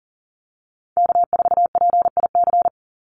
Morse code, heard as a keyed tone, about 40 wpm, 700 Hz. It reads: K4PIC
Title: NA SOTA Chaser Callsigns 40wpm